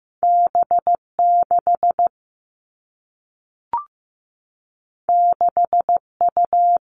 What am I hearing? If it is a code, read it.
Morse code, heard as a keyed tone, about 15 wpm, 700 Hz. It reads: B 6 6 U
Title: Two Challenging Letter-Number Combinations 15wpm